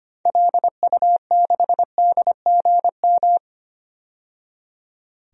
Morse code, heard as a keyed tone, about 25 wpm, 700 Hz. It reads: LU6DGM
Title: DXCC-100 Call Signs - 6 Characters - 25wpm